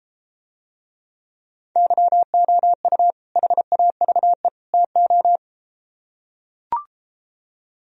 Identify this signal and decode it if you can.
Morse code, heard as a keyed tone, about 33 wpm, 700 Hz. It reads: YOU HAVE TO